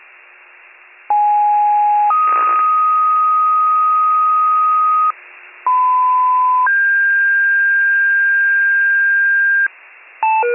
Two-tone page, heard as a one-second tone, then a three-second tone, about 830 then 1250 Hz, about 1010 then 1600 Hz, with noise. It is heard from 1.1 to 9.7 s.